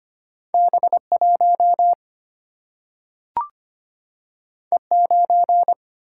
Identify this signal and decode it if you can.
Morse code, heard as a keyed tone, about 25 wpm, 700 Hz. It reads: B 1 E 9
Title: Single-Letter Single-Number 25wpm